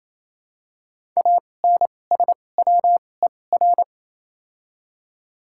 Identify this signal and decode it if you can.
Morse code, heard as a keyed tone, about 28 wpm, spaced as for 20 wpm, 700 Hz. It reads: ANSWER
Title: 2x Character Spacing - Top 500 Words - 28wpm